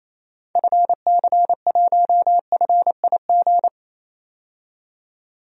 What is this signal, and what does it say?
Morse code, heard as a keyed tone, about 28 wpm, 700 Hz. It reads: FC1FIG
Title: DXCC-100 Call Signs - 6 Characters - 28wpm